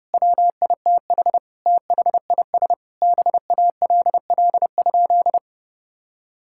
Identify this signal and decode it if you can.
Morse code, heard as a keyed tone, about 30 wpm, 700 Hz. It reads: WITH THIS BALL?
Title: Sentences from Top 1000 Words - Review Part 6 - 30wpm